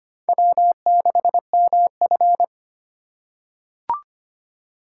Morse code, heard as a keyed tone, about 25 wpm, 700 Hz. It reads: W6MF
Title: USA Call Signs - 4 Characters - 6x Character Spacing - 25wpm